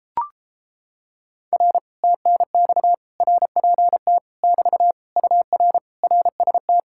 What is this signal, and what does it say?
Morse code, heard as a keyed tone, about 29 wpm, 700 Hz. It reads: R TNX RPT <BT> UR RST